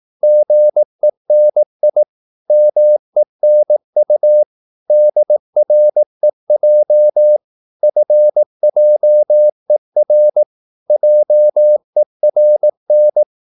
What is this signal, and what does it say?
Morse code, heard as a keyed tone, about 18 wpm, 600 Hz. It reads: GENI MENU DREJ FJER JERN
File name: Korte ord DK 18wpm